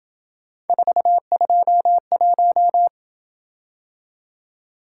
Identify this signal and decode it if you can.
Morse code, heard as a keyed tone, about 26 wpm, 700 Hz. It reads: 421